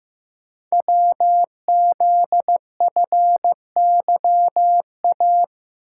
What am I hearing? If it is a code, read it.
Morse code, heard as a keyed tone, about 15 wpm, 700 Hz. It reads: WZFYA
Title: Code Groups - 5 Letters - 15wpm